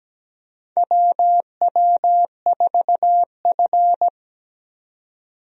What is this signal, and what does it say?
Morse code, heard as a keyed tone, about 17 wpm, 700 Hz. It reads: WW4F
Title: USA Call Signs - 4 Characters - 17wpm